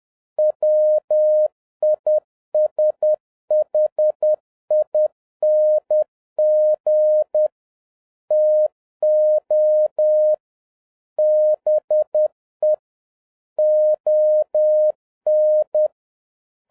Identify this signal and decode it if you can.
Morse code, heard as a keyed tone, about 10 wpm, 610 Hz. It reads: WISHING TO BE ON
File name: 2011-07-mp3 / n41 EXCITING QRP CW OPERATION 10wpm